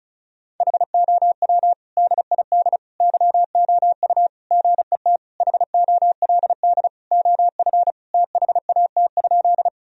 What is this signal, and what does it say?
Morse code, heard as a keyed tone, about 34 wpm, 700 Hz. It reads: HOW DID YOU GET HOLD OF THAT?
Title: Sentences from Top 1000 Words - Review Part 2 - 35wpm